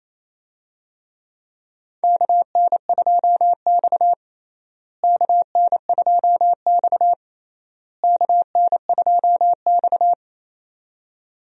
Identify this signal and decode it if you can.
Morse code, heard as a keyed tone, about 28 wpm, 700 Hz. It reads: KN2X KN2X KN2X
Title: USA Call Signs - 4 Characters - Repeated 3x - 28wpm